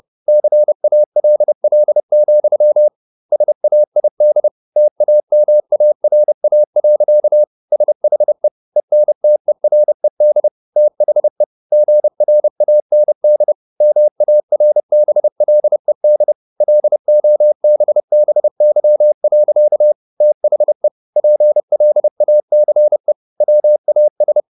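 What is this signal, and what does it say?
Morse code, heard as a keyed tone, about 30 wpm, 600 Hz. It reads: CALL, SAID TAMARA. SHE ENTERED THE GRAND MARBLED LOBBY. THE PLACE WAS
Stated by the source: Morse code at 30 wpm, the novel Never by Ken Follett in English, from CW med Gnister.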